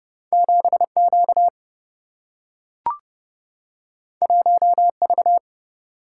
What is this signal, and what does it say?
Morse code, heard as a keyed tone, about 30 wpm, 700 Hz. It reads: 7 Q 1 V
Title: Single-Letter Single-Number 30wpm